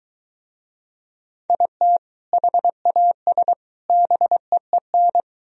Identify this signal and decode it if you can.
Morse code, heard as a keyed tone, about 21 wpm, 700 Hz. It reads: IT HAS BEEN